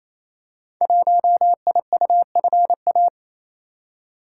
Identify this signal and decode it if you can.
Morse code, heard as a keyed tone, about 29 wpm, 700 Hz. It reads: 1IUFA
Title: Code Groups - 5 Letter:Numbers - 28wpm